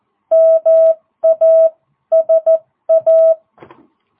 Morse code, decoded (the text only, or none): MASA